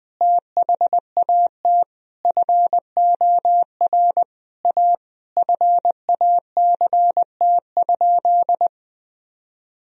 Morse code, heard as a keyed tone, about 21 wpm, 700 Hz. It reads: THAT FOR A FACT?
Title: Sentences from Top 1000 Words - Review Part 3 - 20wpm